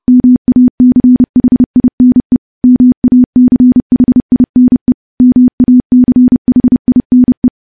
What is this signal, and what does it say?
Morse code, heard as a keyed tone, about 28 wpm, 250 Hz. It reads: MACHINE MACHINE MACHINE
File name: morse / morse-training-30WPM